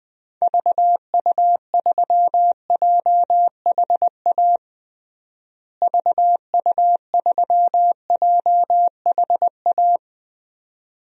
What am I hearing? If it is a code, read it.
Morse code, heard as a keyed tone, about 20 wpm, 700 Hz. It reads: VU3JHA VU3JHA
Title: DXCC-100 Call Signs - 6 Characters - Repeated 2x - 20wpm